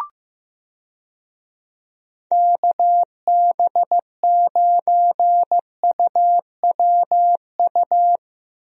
Morse code, heard as a keyed tone, about 15 wpm, 700 Hz. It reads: KB9UWU
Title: USA Call Signs - 6 Characters - Repeated 2x - 15wpm